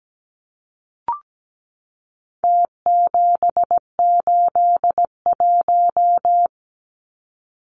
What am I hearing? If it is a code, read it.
Morse code, heard as a keyed tone, about 17 wpm, 700 Hz. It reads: T781
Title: Numbers - Contest Style - 4-Digit 17wpm